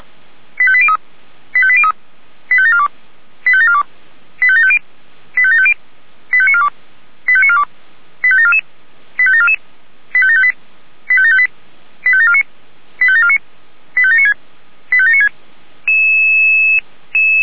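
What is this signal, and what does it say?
Five-tone selective call (ZVEI), heard as quick runs of five tones, tones of about 70 ms. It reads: 85803 85803 85742 85742 85750 85750 85942 85942 85744 85744 85758 85758 85749 85749 85796 85796
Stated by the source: Five-tone ZVEI selective call, SDR receiver, 464 channel.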